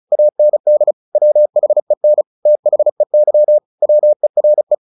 Morse code, heard as a keyed tone, about 34 wpm, 600 Hz. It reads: AND WHEN THEY WERE
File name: the old man and the sea 2 35wpm